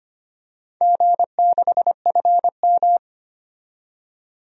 Morse code, heard as a keyed tone, about 25 wpm, 700 Hz. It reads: G6FM
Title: DXCC-100 Call Signs - 4 Characters - 25wpm